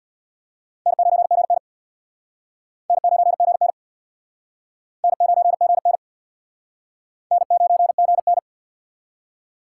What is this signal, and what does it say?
Morse code, heard as a keyed tone, about 65 wpm, 700 Hz. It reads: N9GN N9GN N9GN N9GN